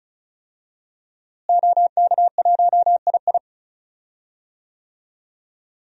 Morse code, heard as a keyed tone, about 36 wpm, 700 Hz. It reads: OK1II
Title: DXCC-100 Call Signs - 5 Characters - Repeated 3x - 35wpm